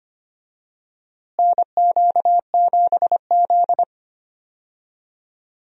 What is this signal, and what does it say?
Morse code, heard as a keyed tone, about 25 wpm, 700 Hz. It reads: NQ7Z
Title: USA Call Signs - 4 Characters - Repeated 3x - 25wpm